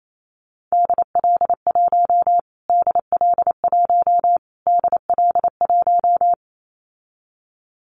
Morse code, heard as a keyed tone, about 28 wpm, 700 Hz. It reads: DL1 DL1 DL1
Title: Callsign Prefixes for Top 25 DXCC - Repeated 3x - 28wpm